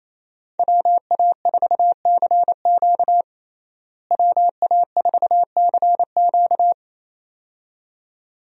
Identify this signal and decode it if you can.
Morse code, heard as a keyed tone, about 28 wpm, 700 Hz. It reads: WA4CQ WA4CQ
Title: USA Call Signs - 5 Characters - Repeated 2x - 28wpm